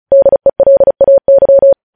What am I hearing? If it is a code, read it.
Morse code, heard as a keyed tone, about 36 wpm, 550 Hz. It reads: DELAY